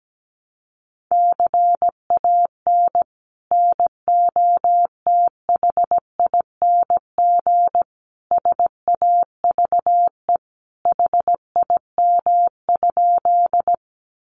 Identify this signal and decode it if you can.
Morse code, heard as a keyed tone, about 17 wpm, 700 Hz. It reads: CAN NOTHING SAVE HIM?
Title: Sentences from Top 1100 Words 17wpm